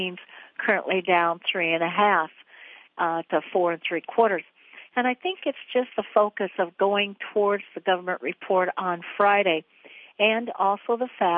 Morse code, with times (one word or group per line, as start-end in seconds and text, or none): none